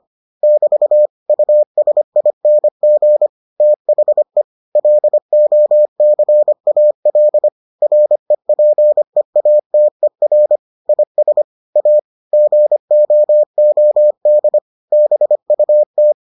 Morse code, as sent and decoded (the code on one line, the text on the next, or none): -...- / ..- ... .. -. --. / - .... / .-.. --- -.-. .- .-.. / .-. .--. .- - . .-. / .. ... / .- / --. --- --- -.. / -... ..- -
<BT> USING THE LOCAL REPEATER IS A GOOD BUT